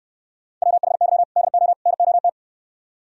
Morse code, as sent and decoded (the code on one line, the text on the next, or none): .-- .... -.-- / -.. --- / -. --- -
WHY DO NOT